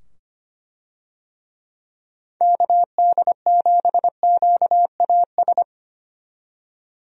Morse code, heard as a keyed tone, about 25 wpm, 700 Hz. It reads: KD7QAS
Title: USA Call Signs - 6 Characters - 25wpm